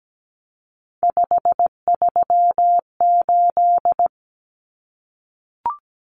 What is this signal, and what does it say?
Morse code, heard as a keyed tone, about 17 wpm, 700 Hz. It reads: 538
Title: Numbers - 3-Digit 17wpm